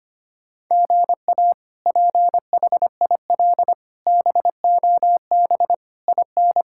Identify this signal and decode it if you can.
Morse code, heard as a keyed tone, about 25 wpm, 700 Hz. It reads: GA PHIL BOB IN
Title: K1USN SST QSO Practice - 25wpm